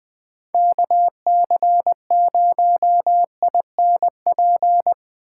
Morse code, heard as a keyed tone, about 20 wpm, 700 Hz. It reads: KC0INP
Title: K1USN SST QSO Practice - 20wpm